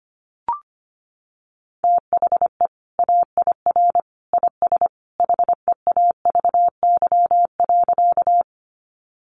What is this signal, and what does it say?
Morse code, heard as a keyed tone, about 25 wpm, 700 Hz. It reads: THE AIR IS HEAVY.
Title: Sentences from Top 1100 Words 25wpm